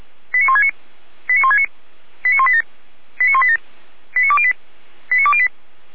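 Five-tone selective call (ZVEI), heard as quick runs of five tones, tones of about 80 ms. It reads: 89169 89169 89187 89187 89208 89208
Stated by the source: Received by SDR; ZVEI five-tone call.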